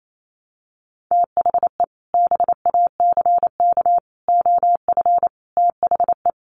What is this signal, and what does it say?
Morse code, heard as a keyed tone, about 28 wpm, 700 Hz. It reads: THE BACK OF THE
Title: Sets of 5 Words Encore 28wpm